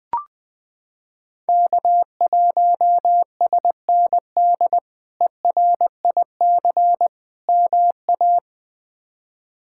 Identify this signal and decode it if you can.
Morse code, heard as a keyed tone, about 21 wpm, 700 Hz. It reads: K1SND ERIC MA
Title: K1USN SST QSO Practice - 20wpm